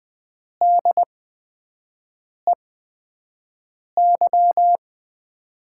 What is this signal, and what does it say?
Morse code, heard as a keyed tone, about 20 wpm, spaced as for 5 wpm, 700 Hz. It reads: DEY
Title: USA Call Signs - 8x Character Spacing - 20wpm